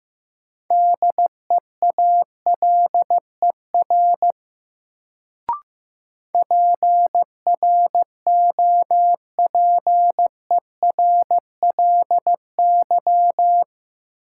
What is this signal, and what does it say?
Morse code, heard as a keyed tone, about 15 wpm, 700 Hz. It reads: DEALER PROPERLY
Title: Sentences from Top 1100 Words - New Word Drill - 15wpm